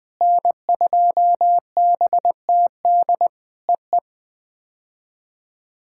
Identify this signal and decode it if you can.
Morse code, heard as a keyed tone, about 20 wpm, 700 Hz. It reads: N2BTD EE
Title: POTA QSO Practice - 20wpm